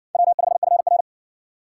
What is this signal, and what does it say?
Morse code, heard as a keyed tone, about 58 wpm, 700 Hz. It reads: W5FR